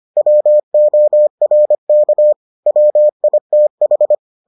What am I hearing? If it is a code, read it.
Morse code, heard as a keyed tone, about 25 wpm, 600 Hz. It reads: WORK WITH